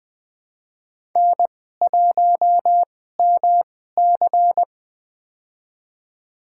Morse code, heard as a keyed tone, about 20 wpm, spaced as for 14 wpm, 700 Hz. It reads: N1MC
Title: USA Call Signs - 4 Characters - 2x Character Spacing - 20wpm